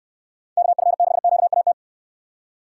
Morse code, heard as a keyed tone, about 69 wpm, 700 Hz. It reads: KF6QAT